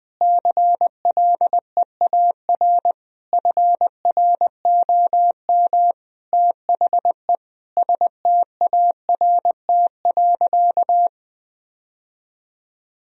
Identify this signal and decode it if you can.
Morse code, heard as a keyed tone, about 20 wpm, 700 Hz. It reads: CLEAR FROM THE START.
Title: Sentences from Top 1000 Words - Review Part 1 - 20wpm